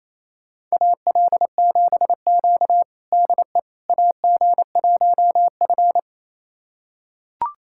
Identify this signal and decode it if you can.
Morse code, heard as a keyed tone, about 28 wpm, 700 Hz. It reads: AL7Q DE AG1F
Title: USA Callsign DE Callsign - 4 Characters - 28wpm